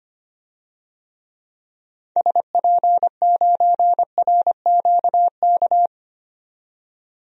Morse code, heard as a keyed tone, about 25 wpm, 700 Hz. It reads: SP9RQK
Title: DXCC-100 Call Signs - 6 Characters - 25wpm